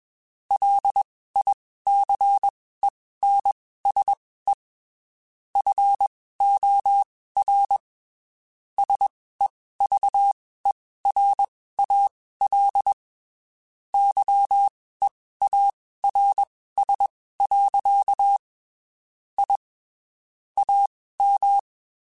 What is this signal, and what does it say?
Morse code, heard as a keyed tone, about 21 wpm, spaced as for 15 wpm, 800 Hz. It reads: LICENSE FOR SEVERAL YEARS. I AM